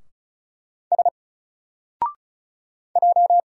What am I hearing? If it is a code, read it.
Morse code, heard as a keyed tone, about 33 wpm, 700 Hz. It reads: S J
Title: Single Letter-Number 35wpm